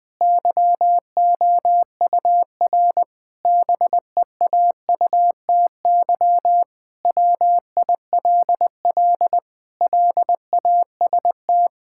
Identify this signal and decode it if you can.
Morse code, heard as a keyed tone, about 20 wpm, 700 Hz. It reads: YOUR BEAUTY WILL LAST
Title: Sentences from Top 1600 Words 20wpm